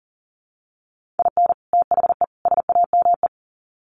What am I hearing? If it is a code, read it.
Morse code, heard as a keyed tone, about 40 wpm, 700 Hz. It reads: IN THE SAME
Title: Sets of 3 Words 40wpm